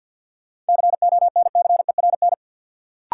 Morse code, heard as a keyed tone, about 53 wpm, 700 Hz. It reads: CONCERN
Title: Sentences from Top 1000 Words - New Word Drill - 50wpm